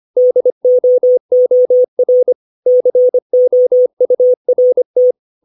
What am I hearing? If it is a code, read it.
Morse code, heard as a keyed tone, about 25 wpm, 500 Hz. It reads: DOOR COURT
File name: 632 words / 632 Words 25WPM